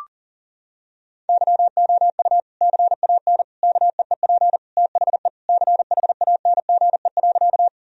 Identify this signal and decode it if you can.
Morse code, heard as a keyed tone, about 40 wpm, 700 Hz. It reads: YOU CAN KEEP THE CHANGE.
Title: Sentences from Top 1000 Words - Review Part 1 - 40wpm